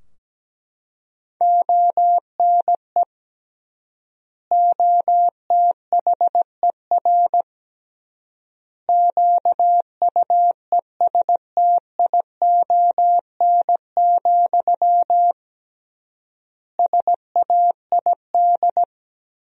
Morse code, heard as a keyed tone, about 17 wpm, 700 Hz. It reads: ONE OTHER QUESTION, SAID
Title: 3x Word-Spacing - Sentences from Top 400 Words - 17wpm